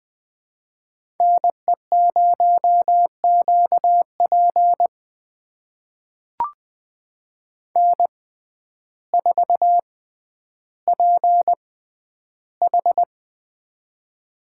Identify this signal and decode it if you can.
Morse code, heard as a keyed tone, about 20 wpm, spaced as for 11 wpm, 700 Hz. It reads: NE0QP N4PH
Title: USA Call Signs - 6x Character Spacing - 20wpm